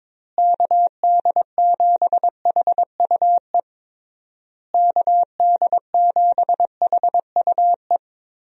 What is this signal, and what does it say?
Morse code, heard as a keyed tone, about 22 wpm, 700 Hz. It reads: KD7HUE KD7HUE
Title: USA Call Signs - 6 Characters - Repeated 3x - 22wpm